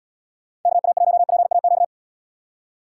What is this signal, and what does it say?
Morse code, heard as a keyed tone, about 78 wpm, 700 Hz. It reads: KA1PAQ